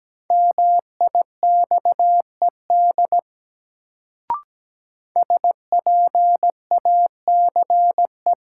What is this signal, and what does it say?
Morse code, heard as a keyed tone, about 17 wpm, 700 Hz. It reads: MIXED SPACE
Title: Sentences from Top 1500 Words - New Word Drill Alt - 17wpm